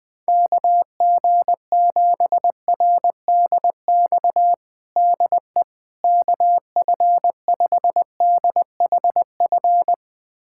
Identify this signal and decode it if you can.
Morse code, heard as a keyed tone, about 20 wpm, 700 Hz. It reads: KG7RDX DE KF5DHF
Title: USA Callsign DE Callsign - 6 Characters - 20wpm